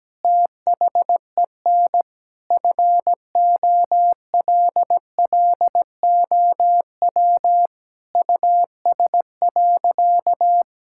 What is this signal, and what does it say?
Morse code, heard as a keyed tone, about 17 wpm, 700 Hz. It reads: THEN FOLLOW US.
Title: Sentences from Top 1000 Words - Review Part 6 - 17wpm